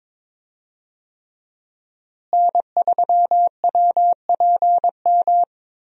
Morse code, heard as a keyed tone, about 21 wpm, 700 Hz. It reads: N3WPM
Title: USA Call Signs - 5 Characters - Repeated 3x - 22wpm